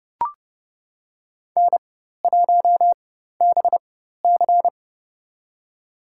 Morse code, heard as a keyed tone, about 31 wpm, spaced as for 14 wpm, 700 Hz. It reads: N1BC
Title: USA Call Signs - 4 Characters - 4x Character Spacing - 30wpm